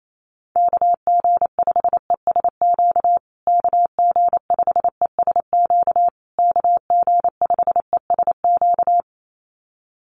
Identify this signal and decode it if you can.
Morse code, heard as a keyed tone, about 28 wpm, 700 Hz. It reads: KG5ESQ KG5ESQ KG5ESQ
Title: USA Call Signs Repeated 3x 28wpm